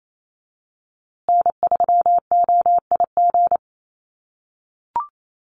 Morse code, heard as a keyed tone, about 28 wpm, 700 Hz. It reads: N3OIG
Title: USA Call Signs Repeated 2x 28wpm